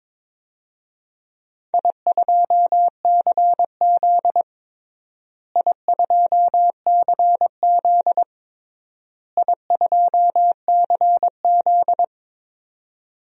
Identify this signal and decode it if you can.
Morse code, heard as a keyed tone, about 22 wpm, 700 Hz. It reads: I2CZ I2CZ I2CZ
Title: DXCC-100 Call Signs - 4 Characters - Repeated 3x - 22wpm